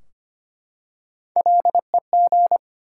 Morse code, heard as a keyed tone, about 25 wpm, 700 Hz. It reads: LEG
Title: Three Letter Words - 25wpm